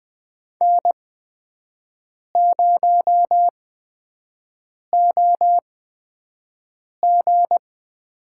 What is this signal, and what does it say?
Morse code, heard as a keyed tone, about 20 wpm, spaced as for 5 wpm, 700 Hz. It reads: N0OG
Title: USA Call Signs - 5 Characters - 8x Character Spacing - 20wpm